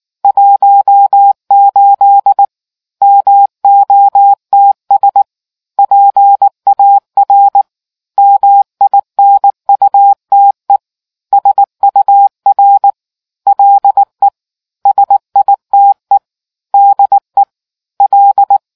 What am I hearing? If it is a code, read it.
Morse code, heard as a keyed tone, about 19 wpm, 800 Hz. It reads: 18 MOTS PAR MINUTE SUR LE SITE DE L